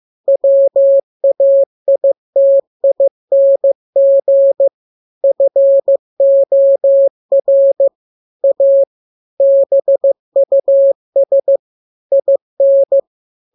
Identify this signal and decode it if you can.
Morse code, heard as a keyed tone, about 15 wpm, 550 Hz. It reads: WAITING FOR A BUS IN
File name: jokes / joke 25 15 0000